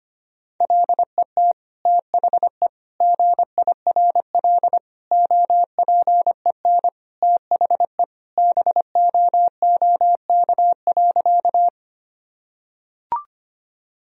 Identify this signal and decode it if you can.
Morse code, heard as a keyed tone, about 25 wpm, 700 Hz. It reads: LET THE GIRL OPEN THE BOOK.
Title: Sentences from Top 1000 Words - Review Part 1 - 25wpm